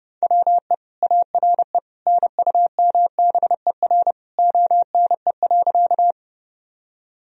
Morse code, heard as a keyed tone, about 30 wpm, 700 Hz. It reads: WE ARE NUMBER ONE.
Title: Sentences from Top 1000 Words - Review Part 1 - 30wpm